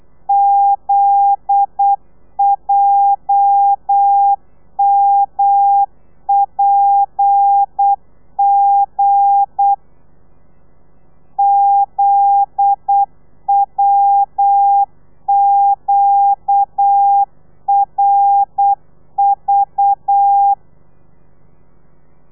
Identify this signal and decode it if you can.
Morse code, heard as a keyed tone, about 8 wpm, 800 Hz. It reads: ZJMPG ZWQRV
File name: telegrafia / i2tfj / Lez9 35